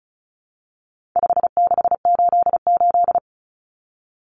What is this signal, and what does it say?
Morse code, heard as a keyed tone, about 34 wpm, 700 Hz. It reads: 5688
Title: Common Cut Numbers - 4-Digit 35wpm